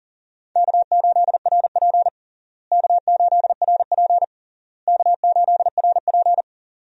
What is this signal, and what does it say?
Morse code, heard as a keyed tone, about 40 wpm, 700 Hz. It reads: K8RP K8RP K8RP
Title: USA Call Signs - 4 Characters - Repeated 3x - 40wpm